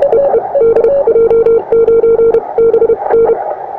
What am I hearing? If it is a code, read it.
Morse code, heard as a keyed tone, about 32 wpm, spaced as for 26 wpm, 460 Hz. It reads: EE DJ9BN
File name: attachments / europe 20m